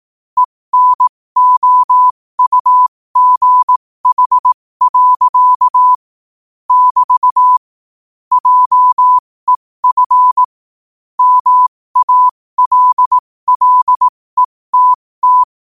Morse code, heard as a keyed tone, about 18 wpm, spaced as for 15 wpm, 1000 Hz. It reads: ENOUGH. <BT> JEF MALLETT
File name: qotdfiles / 15wpm